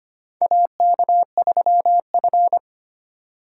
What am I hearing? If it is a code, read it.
Morse code, heard as a keyed tone, about 25 wpm, 700 Hz. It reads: AK3F
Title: USA Call Signs - 4 Characters - 4x Character Spacing - 25wpm